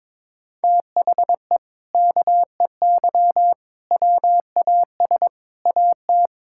Morse code, heard as a keyed tone, about 22 wpm, 700 Hz. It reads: THE KEY WAS AT
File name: practice / 22wpm / Sentences with 4-Word Phrases - 22wpm